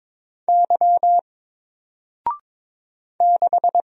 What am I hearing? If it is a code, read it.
Morse code, heard as a keyed tone, about 22 wpm, 700 Hz. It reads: Y 6
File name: Single-Letter-Number-22wpm